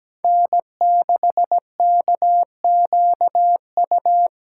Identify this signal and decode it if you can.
Morse code, heard as a keyed tone, about 20 wpm, 700 Hz. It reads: N6KQU
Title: Speed Racing - USA Call Signs 17wpm